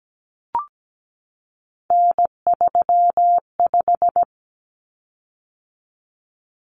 Morse code, heard as a keyed tone, about 17 wpm, 700 Hz. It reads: N35
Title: Common Cut Numbers - 3-Digit 17wpm